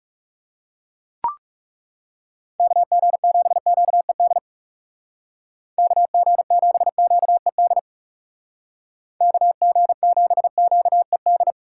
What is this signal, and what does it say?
Morse code, heard as a keyed tone, about 36 wpm, 700 Hz. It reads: KG7QED KG7QED KG7QED
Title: Speed Racing - USA 2x3 Call Signs 30wpm